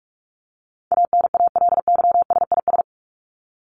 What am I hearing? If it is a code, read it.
Morse code, heard as a keyed tone, about 44 wpm, 700 Hz. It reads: ANALYSIS
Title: Top 501-750 Words 45wpm